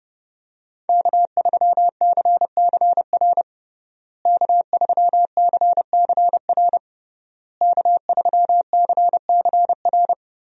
Morse code, heard as a keyed tone, about 30 wpm, 700 Hz. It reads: K3CCR K3CCR K3CCR
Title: USA Call Signs - 5 Characters - Repeated 3x - 30wpm